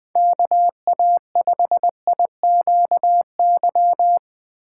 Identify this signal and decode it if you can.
Morse code, heard as a keyed tone, about 20 wpm, 700 Hz. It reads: KA5IQY
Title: USA Call Signs - 6 Characters - 2x Character Spacing - 20wpm